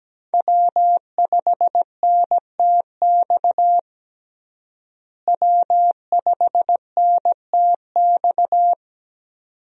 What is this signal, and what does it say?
Morse code, heard as a keyed tone, about 17 wpm, 700 Hz. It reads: W5NTX W5NTX
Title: USA Call Signs - 5 Characters - Repeated 2x - 17wpm